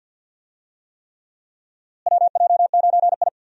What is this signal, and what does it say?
Morse code, heard as a keyed tone, about 51 wpm, 700 Hz. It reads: WJ9I
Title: USA Call Signs - 4 Characters - Repeated 3x - 50wpm